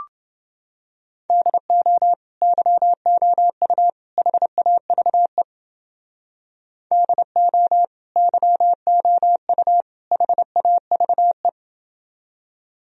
Morse code, heard as a keyed tone, about 28 wpm, 700 Hz. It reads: DO YOU HAVE DO YOU HAVE